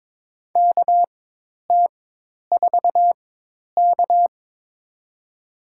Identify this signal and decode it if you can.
Morse code, heard as a keyed tone, about 22 wpm, spaced as for 10 wpm, 700 Hz. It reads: KT4K
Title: USA Call Signs - 4x Character Spacing - 22wpm